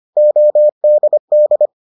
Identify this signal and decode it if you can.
Morse code, heard as a keyed tone, about 25 wpm, 600 Hz. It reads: ODD